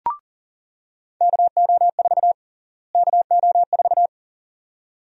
Morse code, heard as a keyed tone, about 40 wpm, 700 Hz. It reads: KO4 KO4
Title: USA Call Sign Prefixes - Repeated 2x - 40wpm